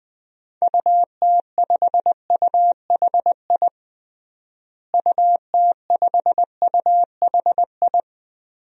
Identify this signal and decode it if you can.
Morse code, heard as a keyed tone, about 20 wpm, 700 Hz. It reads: UT5UHI UT5UHI
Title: DXCC-100 Call Signs - 6 Characters - Repeated 2x - 20wpm